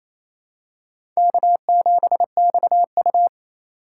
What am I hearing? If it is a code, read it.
Morse code, heard as a keyed tone, about 28 wpm, 700 Hz. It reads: K7XU
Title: USA Call Signs - 2x Character Spacing - 28wpm